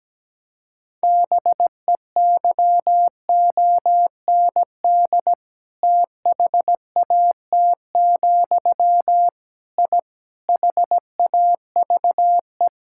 Morse code, heard as a keyed tone, about 17 wpm, 700 Hz. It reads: BEYOND THAT, I HAVE
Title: Sentences from Top 1000 Words - Review Part 6 - 17wpm